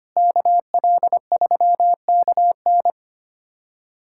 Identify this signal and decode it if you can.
Morse code, heard as a keyed tone, about 25 wpm, 700 Hz. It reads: KL3KN
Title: USA Callsign DE Callsign - 5 Characters - 25wpm